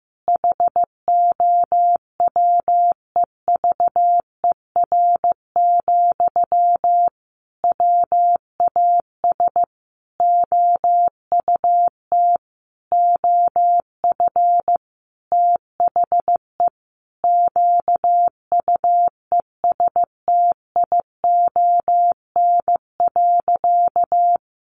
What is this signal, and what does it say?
Morse code, heard as a keyed tone, about 15 wpm, 700 Hz. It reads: HOWEVER, WAS OUT OF THE QUESTION.
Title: Sentences from Top 600 Words 15wpm